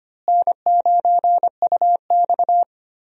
Morse code, heard as a keyed tone, about 25 wpm, 700 Hz. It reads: N9UX
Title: USA Call Signs - 4 Characters - Repeated 3x - 25wpm